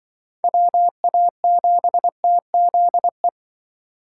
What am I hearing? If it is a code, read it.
Morse code, heard as a keyed tone, about 23 wpm, 700 Hz. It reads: WA7TZE
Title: Speed Racing - USA Call Signs 20wpm